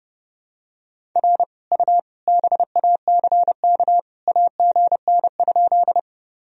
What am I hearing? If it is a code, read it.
Morse code, heard as a keyed tone, about 30 wpm, 700 Hz. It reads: R U BACK AGN?
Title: Sentences from Top 200 Words and Abbreviations 30wpm